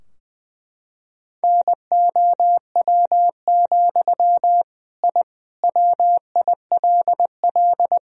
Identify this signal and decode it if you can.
Morse code, heard as a keyed tone, about 20 wpm, 700 Hz. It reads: NOW, I WILL
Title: Sentences from Top 1000 Words - Review Part 3 - 20wpm